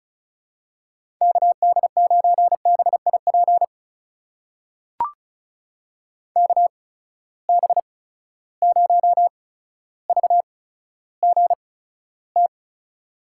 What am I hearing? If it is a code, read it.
Morse code, heard as a keyed tone, about 36 wpm, spaced as for 15 wpm, 700 Hz. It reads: KD9BIP KB0VGT
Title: USA Call Signs - 6 Characters - 8x Character Spacing - 35wpm